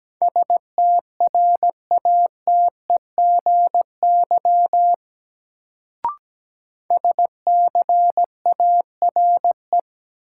Morse code, heard as a keyed tone, about 17 wpm, 700 Hz. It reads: STRATEGY SCARE
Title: Sentences from Top 1000 Words - Review Part 5 - New Word Drill - 17wpm